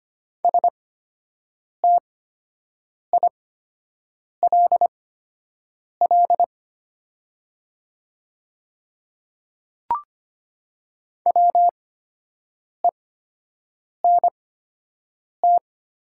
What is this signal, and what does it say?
Morse code, heard as a keyed tone, about 25 wpm, spaced as for 7 wpm, 700 Hz. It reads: STILL WENT
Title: 8x Character Spacing - Top 500 Words - 25wpm